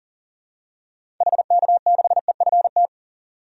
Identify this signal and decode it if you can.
Morse code, heard as a keyed tone, about 40 wpm, 700 Hz. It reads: HK6EFT
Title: DXCC-100 Call Signs - 6 Characters - 40wpm